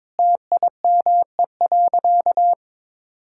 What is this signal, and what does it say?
Morse code, heard as a keyed tone, about 22 wpm, 700 Hz. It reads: TIME.
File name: practice / 22wpm / Sentences with 3-Word Phrases - 22wpm